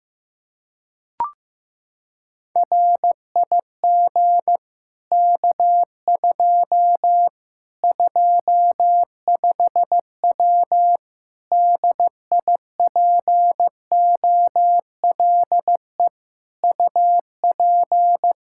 Morse code, heard as a keyed tone, about 15 wpm, 700 Hz. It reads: RIG K2 25W DIPOLE UP
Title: Practice QSO - Exchange 2 - Info - 15wpm